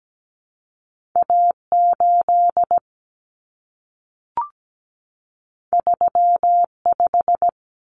Morse code, heard as a keyed tone, about 17 wpm, 700 Hz. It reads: A8 35
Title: Common Cut Numbers - 2-Digit 17wpm